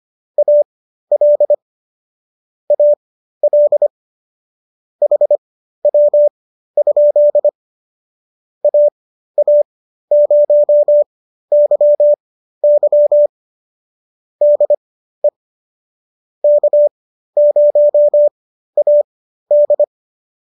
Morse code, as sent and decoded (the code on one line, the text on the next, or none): .- .-.. / .- .-.. / .... .-- ..--.. / .- .- ----- -.-- -.-- / -.. / -.- ----- .- -..
AL AL HW? AA0YY DE K0AD